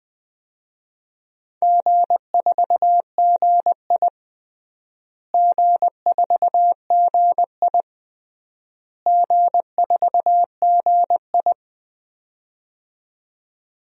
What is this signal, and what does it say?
Morse code, heard as a keyed tone, about 21 wpm, 700 Hz. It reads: G4GI G4GI G4GI
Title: DXCC-100 Call Signs - 4 Characters - Repeated 3x - 20wpm